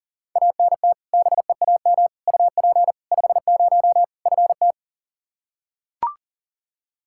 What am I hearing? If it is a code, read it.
Morse code, heard as a keyed tone, about 40 wpm, 700 Hz. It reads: ANT BEAM UP 50 FT
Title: Practice QSO - Exchange 2 - Info - 40wpm